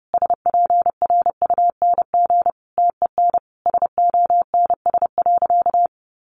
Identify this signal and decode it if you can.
Morse code, heard as a keyed tone, about 30 wpm, 700 Hz. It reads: SPRUNG TEN SONS.
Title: A Sound Of Thunder 30wpm